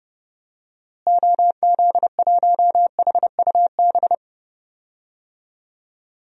Morse code, heard as a keyed tone, about 32 wpm, 700 Hz. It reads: OZ1HUB
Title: DXCC-100 Call Signs - 6 Characters - Repeated 2x - 30wpm